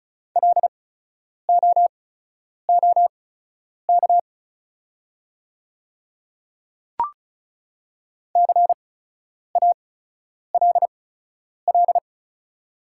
Morse code, heard as a keyed tone, about 36 wpm, spaced as for 10 wpm, 700 Hz. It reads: LOOK CALL
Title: 8x Character Spacing - Top 100 Words - 35wpm